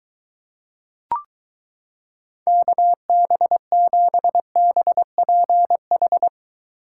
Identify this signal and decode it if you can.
Morse code, heard as a keyed tone, about 23 wpm, 700 Hz. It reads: KB7BPH